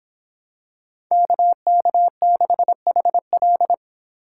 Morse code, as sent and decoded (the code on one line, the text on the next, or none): -.- -.- -.... .... .-..
KK6HL